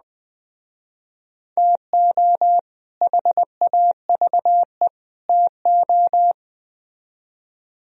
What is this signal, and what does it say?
Morse code, heard as a keyed tone, about 20 wpm, 700 Hz. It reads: TO HAVE TO